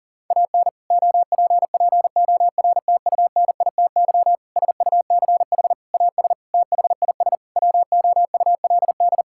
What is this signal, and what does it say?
Morse code, heard as a keyed tone, about 40 wpm, 700 Hz. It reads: AN OPPORTUNITY SUCH AS THIS WOULD
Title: Sentences from Top 1600 Words 40wpm